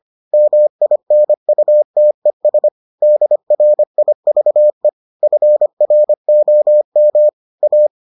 Morse code, heard as a keyed tone, about 25 wpm, 600 Hz. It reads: MINUTES DRIVE FROM A